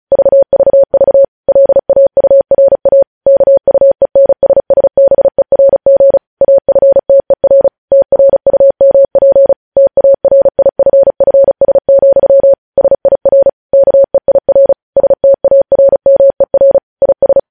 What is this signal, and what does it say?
Morse code, heard as a keyed tone, about 36 wpm, 550 Hz. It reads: VVV LAURA KUENSSBERG AFTER TRUMP TARIFFS, SIR KEIR STARMER IS
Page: News Headlines in Morse Code at 35 WPM